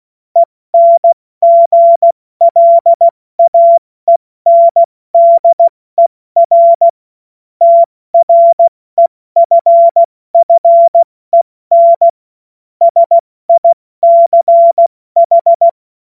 Morse code, heard as a keyed tone, about 16 wpm, spaced as for 14 wpm, 680 Hz. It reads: ENGLAENDER TREFFEN SICH
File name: Morsekurs / Witze / Witz 15. Februar 2022 70 F80